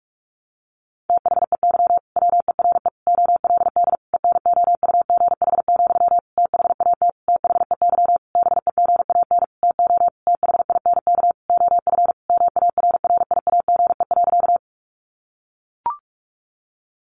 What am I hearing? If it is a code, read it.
Morse code, heard as a keyed tone, about 44 wpm, 700 Hz. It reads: THEY WERE OLD ENOUGH, THAT THEY BEGAN TO THINK OF MARRIAGE.
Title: Sentences from Top 800 Words 45wpm